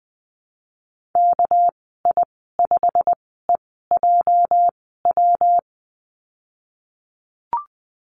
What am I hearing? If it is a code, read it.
Morse code, heard as a keyed tone, about 20 wpm, spaced as for 15 wpm, 700 Hz. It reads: KI5EJW
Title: 231 Lesson 37g - Letter x - Call Signs 20wpm